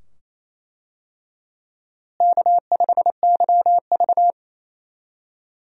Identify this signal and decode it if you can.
Morse code, heard as a keyed tone, about 27 wpm, 700 Hz. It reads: K5YV
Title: USA Call Signs - 4 Characters - Repeated 2x - 28wpm